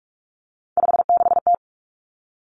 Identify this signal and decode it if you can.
Morse code, heard as a keyed tone, about 46 wpm, 700 Hz. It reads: 56T
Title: Three Challenging Letter-Number Combinations 45wpm